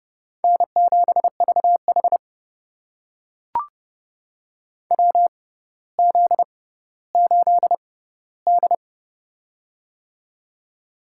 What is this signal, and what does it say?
Morse code, heard as a keyed tone, about 30 wpm, spaced as for 15 wpm, 700 Hz. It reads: N7VH WZ8D
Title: USA Call Signs - 4 Characters - 6x Character Spacing - 30wpm